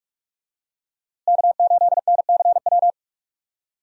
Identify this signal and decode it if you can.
Morse code, heard as a keyed tone, about 46 wpm, 700 Hz. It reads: K8NCW
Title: USA Call Signs - 5 Characters - Repeated 2x - 45wpm